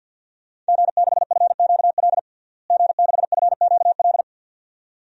Morse code, heard as a keyed tone, about 48 wpm, 700 Hz. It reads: G6FQL G6FQL
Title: DXCC-100 Call Signs - 5 Characters - Repeated 2x - 50wpm